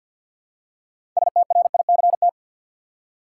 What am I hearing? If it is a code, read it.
Morse code, heard as a keyed tone, about 52 wpm, 700 Hz. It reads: STRICT
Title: Six Letter Words - 50wpm